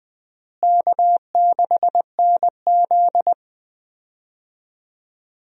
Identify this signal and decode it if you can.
Morse code, heard as a keyed tone, about 20 wpm, 700 Hz. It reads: K6NZ
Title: USA Call Signs - 4 Characters - Repeated 3x - 20wpm